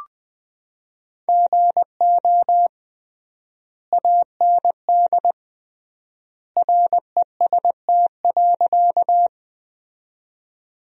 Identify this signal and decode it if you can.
Morse code, heard as a keyed tone, about 21 wpm, 700 Hz. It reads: GO AND REST.
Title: 3x Word-Spacing - Sentences from Top 500 Words - 20wpm